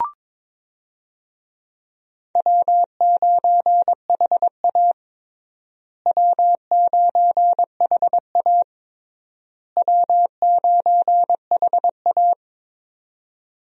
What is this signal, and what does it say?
Morse code, heard as a keyed tone, about 22 wpm, 700 Hz. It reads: W9HA W9HA W9HA
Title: USA Call Signs - 4 Characters - Repeated 3x - 22wpm